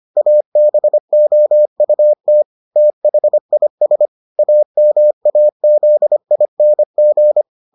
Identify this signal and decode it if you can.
Morse code, heard as a keyed tone, about 25 wpm, 600 Hz. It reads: ABOUT THIS AMAZING